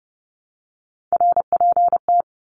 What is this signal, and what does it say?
Morse code, heard as a keyed tone, about 30 wpm, 700 Hz. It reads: RPT
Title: Abbreviations Repeated 3x - 30wpm